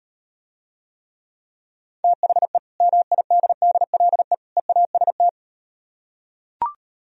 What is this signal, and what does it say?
Morse code, heard as a keyed tone, about 38 wpm, 700 Hz. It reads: THE MIDDLE EAST